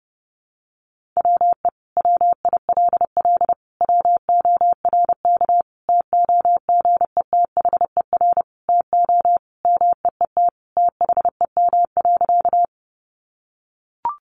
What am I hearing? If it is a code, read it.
Morse code, heard as a keyed tone, about 30 wpm, 700 Hz. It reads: WE WILL WORK TOGETHER TO MEET THEM.
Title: Sentences from Top 1100 Words 30wpm